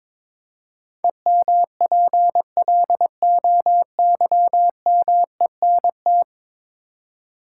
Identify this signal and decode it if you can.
Morse code, heard as a keyed tone, about 22 wpm, 700 Hz. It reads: EMPLOYMENT
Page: morse-code-ninja-practice-22wpm